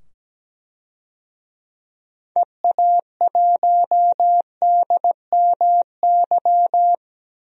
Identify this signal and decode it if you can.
Morse code, heard as a keyed tone, about 17 wpm, 700 Hz. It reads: EA1DMY
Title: DXCC-100 Call Signs - 6 Characters - 17wpm